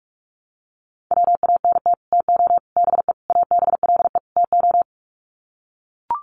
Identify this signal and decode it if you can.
Morse code, heard as a keyed tone, about 44 wpm, 700 Hz. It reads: WANT TO BE ABLE TO
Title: Sets of 5 Words 45wpm